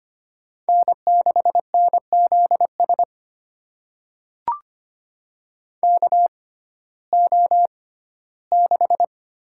Morse code, heard as a keyed tone, about 25 wpm, spaced as for 15 wpm, 700 Hz. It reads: N6NZS KO6
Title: USA Call Signs - 5 Characters - 6x Character Spacing - 25wpm